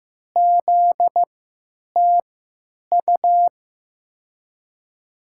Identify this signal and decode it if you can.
Morse code, heard as a keyed tone, about 15 wpm, spaced as for 9 wpm, 700 Hz. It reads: ZTU
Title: USA Call Signs - 3x Character Spacing - 15wpm